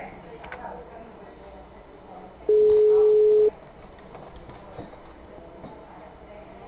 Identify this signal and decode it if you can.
Ringback tone (Europe, 425 Hz), with speech.
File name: out- -2014- - - .35443